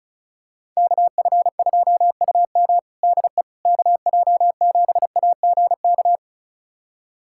Morse code, heard as a keyed tone, about 36 wpm, 700 Hz. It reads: KF2UM DE KJ7AGK
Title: USA Callsign DE Callsign 35wpm